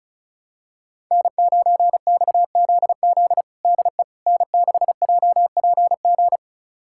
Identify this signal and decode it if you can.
Morse code, heard as a keyed tone, about 34 wpm, 700 Hz. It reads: N9XZZ DE N6JPG
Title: USA Callsign DE Callsign - 5 Characters - 35wpm